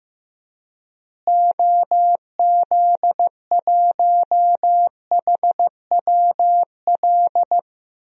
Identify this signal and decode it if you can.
Morse code, heard as a keyed tone, about 15 wpm, 700 Hz. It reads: OZ1HWL
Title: DXCC-100 Call Signs - 6 Characters - Repeated 3x - 15wpm